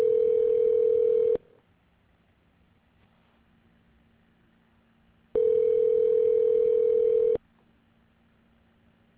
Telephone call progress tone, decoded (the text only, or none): ringback tone